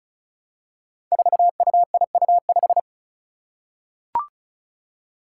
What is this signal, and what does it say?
Morse code, heard as a keyed tone, about 34 wpm, 700 Hz. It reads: 4UIU5
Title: Five Challenging Letter-Number Combinations 35wpm